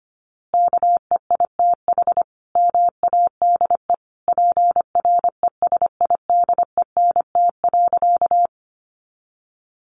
Morse code, heard as a keyed tone, about 25 wpm, 700 Hz. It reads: KEITH MADE PRESIDENT.
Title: A Sound Of Thunder 25wpm